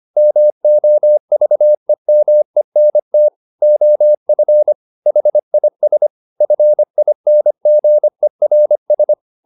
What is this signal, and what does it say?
Morse code, heard as a keyed tone, about 25 wpm, 600 Hz. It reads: MOVEMENT OF HIS FINGERS
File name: the old man and the sea part 1 25wpm